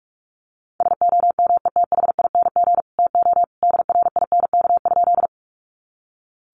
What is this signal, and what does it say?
Morse code, heard as a keyed tone, about 44 wpm, 700 Hz. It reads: SOMETHING TO DRINK?
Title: Sentences from Top 1500 Words 45wpm